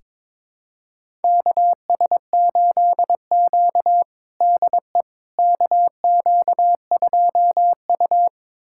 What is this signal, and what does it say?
Morse code, heard as a keyed tone, about 22 wpm, 700 Hz. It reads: KS8Q DE KQ2U
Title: USA Callsign DE Callsign - 4 Characters - 22wpm